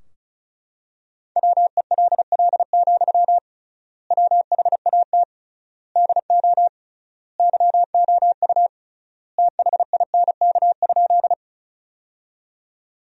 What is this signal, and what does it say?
Morse code, heard as a keyed tone, about 36 wpm, 700 Hz. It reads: WELL, WHAT DO YOU THINK?
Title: 3x Word-Spacing - Sentences from Top 200 Words - 35wpm